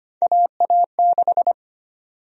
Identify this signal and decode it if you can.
Morse code, heard as a keyed tone, about 25 wpm, 700 Hz. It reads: AA6
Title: USA Call Sign Prefixes - 25wpm